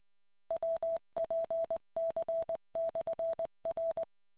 Morse code, heard as a keyed tone, about 20 wpm, 660 Hz. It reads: WPC/R